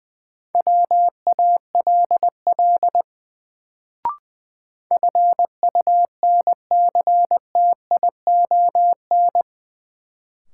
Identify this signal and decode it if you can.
Morse code, heard as a keyed tone, about 20 wpm, 700 Hz. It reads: WALL FUNCTION
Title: Sentences from Top 1100 Words - New Word Drill Alt - 20wpm